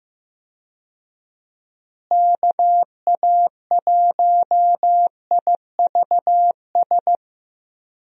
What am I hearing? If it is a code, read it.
Morse code, heard as a keyed tone, about 15 wpm, 700 Hz. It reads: KA1IVS
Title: USA Call Signs - 6 Characters - Repeated 2x - 15wpm